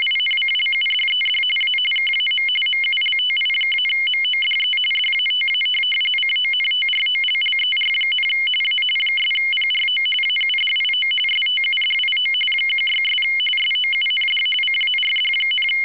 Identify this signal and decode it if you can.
RTTY, about 45.45 baud. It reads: INCLUDED A KAM MULTI-MODE DATA CONTROLLER, A 'PRO AUDIO STUDIO 16' SOUND CARD,
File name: w ry850 / rtty45